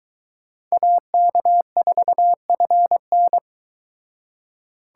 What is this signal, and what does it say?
Morse code, heard as a keyed tone, about 23 wpm, 700 Hz. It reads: AK4FN